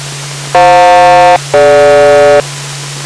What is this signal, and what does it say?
Aircraft SELCAL, heard as two pairs of tones. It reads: FJ-DG